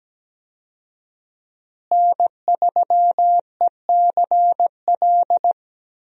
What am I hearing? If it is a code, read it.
Morse code, heard as a keyed tone, about 17 wpm, 700 Hz. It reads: N3ECL
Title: USA Call Signs - 5 Characters - Repeated 3x - 17wpm